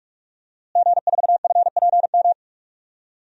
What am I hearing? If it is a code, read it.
Morse code, heard as a keyed tone, about 46 wpm, 700 Hz. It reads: G4FPM
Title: DXCC-100 Call Signs - 5 Characters - Repeated 2x - 45wpm